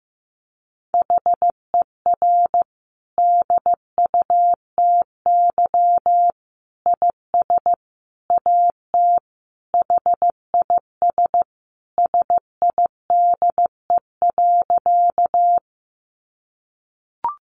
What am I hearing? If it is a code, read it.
Morse code, heard as a keyed tone, about 15 wpm, 700 Hz. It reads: HER DUTY IS AT HIS SIDE.
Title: Sentences from Top 1500 Words 15wpm